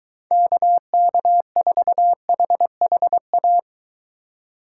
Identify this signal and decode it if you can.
Morse code, heard as a keyed tone, about 22 wpm, 700 Hz. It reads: KK4HHA